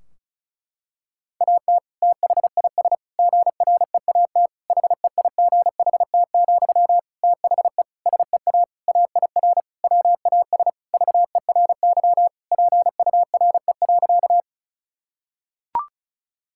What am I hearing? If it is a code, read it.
Morse code, heard as a keyed tone, about 36 wpm, 700 Hz. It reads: AT THIS GREAT HEIGHT, THE SEA AIR WAS VERY PURE.
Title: Sentences from Top 1600 Words 35wpm